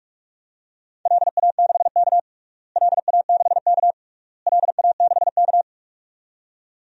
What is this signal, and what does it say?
Morse code, heard as a keyed tone, about 45 wpm, 700 Hz. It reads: LA6K LA6K LA6K
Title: DXCC-100 Call Signs - 4 Characters - Repeated 3x - 45wpm